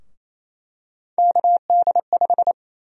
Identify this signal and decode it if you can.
Morse code, heard as a keyed tone, about 28 wpm, 700 Hz. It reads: KD5
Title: USA Call Sign Prefixes - Repeated 2x - 28wpm